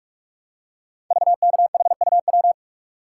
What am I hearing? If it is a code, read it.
Morse code, heard as a keyed tone, about 44 wpm, 700 Hz. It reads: VKHUW
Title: Code Groups - 5 Letters - 45wpm